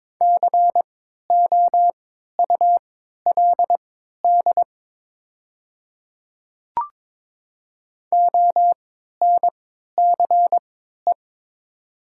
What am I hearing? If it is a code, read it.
Morse code, heard as a keyed tone, about 22 wpm, spaced as for 12 wpm, 700 Hz. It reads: COULD ONCE
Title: 3x Character Spacing - Top 300 Words - 22wpm